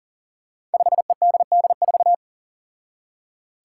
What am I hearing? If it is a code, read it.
Morse code, heard as a keyed tone, about 37 wpm, 700 Hz. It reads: 5EDD4
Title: Five Challenging Letter-Number Combinations 40wpm